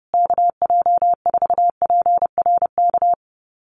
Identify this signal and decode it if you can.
Morse code, heard as a keyed tone, about 30 wpm, 700 Hz. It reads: KJ4PRK
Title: USA Call Signs Repeated 3x 30wpm